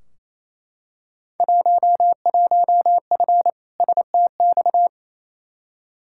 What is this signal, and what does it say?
Morse code, heard as a keyed tone, about 28 wpm, 700 Hz. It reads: 11F STX
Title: Field Day Exchanges Encore - 28wpm